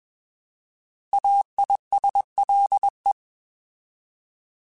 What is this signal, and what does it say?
Morse code, heard as a keyed tone, about 21 wpm, 800 Hz. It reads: AISLE
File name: Alphabetical / Alphabetical mp3s / As @20